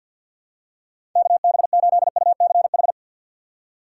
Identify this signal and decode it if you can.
Morse code, heard as a keyed tone, about 52 wpm, 700 Hz. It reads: KB8UCH
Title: USA Call Signs - 4x Character Spacing - 50wpm